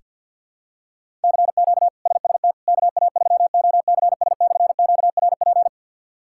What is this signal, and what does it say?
Morse code, heard as a keyed tone, about 46 wpm, 700 Hz. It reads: CQ SST KA3OCS/QRP
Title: K1USN SST QSO Practice - 50wpm